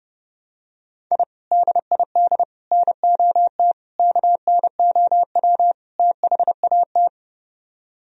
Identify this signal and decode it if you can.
Morse code, heard as a keyed tone, about 32 wpm, 700 Hz. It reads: I DID NOT KNOW THAT